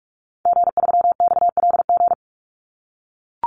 Morse code, heard as a keyed tone, about 44 wpm, 700 Hz. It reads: G3XLG
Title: Call Signs - 45wpm